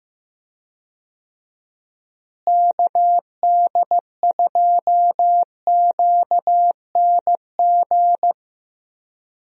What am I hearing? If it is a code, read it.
Morse code, heard as a keyed tone, about 15 wpm, 700 Hz. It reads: KD2QNG
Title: USA Call Signs - 6 Characters - Repeated 3x - 15wpm